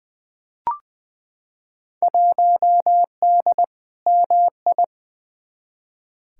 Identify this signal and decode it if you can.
Morse code, heard as a keyed tone, about 21 wpm, 700 Hz. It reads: 1D MI
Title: Field Day Exchanges - 20wpm